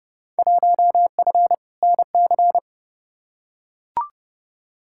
Morse code, heard as a keyed tone, about 30 wpm, 700 Hz. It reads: 1F NC
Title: Field Day Exchanges - 30wpm